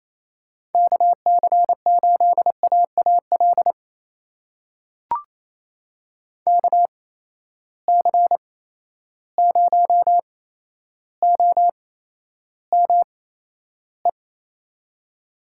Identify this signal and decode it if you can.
Morse code, heard as a keyed tone, about 28 wpm, spaced as for 12 wpm, 700 Hz. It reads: KC8AAL KC0OME
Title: USA Call Signs - 6 Characters - 8x Character Spacing - 28wpm